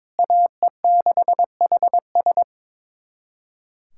Morse code, heard as a keyed tone, about 22 wpm, 700 Hz. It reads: AE6HS
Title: USA Callsign DE Callsign - 5 Characters - 22wpm